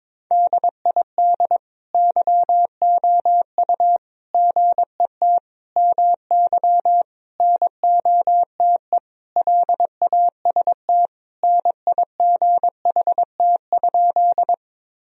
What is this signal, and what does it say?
Morse code, heard as a keyed tone, about 22 wpm, 700 Hz. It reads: DID YOU GET MY NOTE LAST NIGHT?
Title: Sentences from Top 1000 Words - Review Part 2 - 22wpm